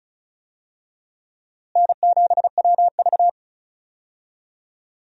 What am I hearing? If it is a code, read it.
Morse code, heard as a keyed tone, about 35 wpm, 700 Hz. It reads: N7WV
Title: USA Call Signs - 4 Characters - Repeated 2x - 35wpm